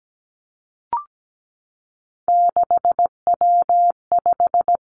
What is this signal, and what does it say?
Morse code, heard as a keyed tone, about 17 wpm, 700 Hz. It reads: 6W5
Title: Three Challenging Letter-Number Combinations 17wpm